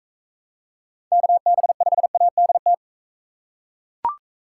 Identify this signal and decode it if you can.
Morse code, heard as a keyed tone, about 38 wpm, 700 Hz. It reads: KB5ADT